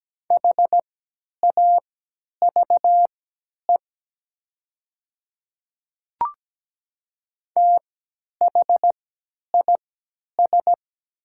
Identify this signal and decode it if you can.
Morse code, heard as a keyed tone, about 17 wpm, spaced as for 10 wpm, 700 Hz. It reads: HAVE THIS
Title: 3x Character Spacing - Top 100 Words - 17wpm